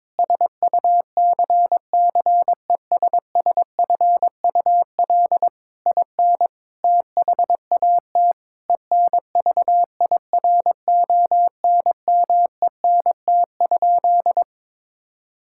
Morse code, heard as a keyed tone, about 22 wpm, 700 Hz. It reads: SUCCESSFUL IN THAT ENVIRONMENT?
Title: Sentences from Top 1000 Words - Review Part 5 - 22wpm